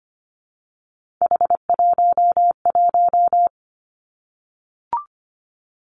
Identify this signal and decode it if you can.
Morse code, heard as a keyed tone, about 25 wpm, 700 Hz. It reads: H11
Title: Three Challenging Letter-Number Combinations 25wpm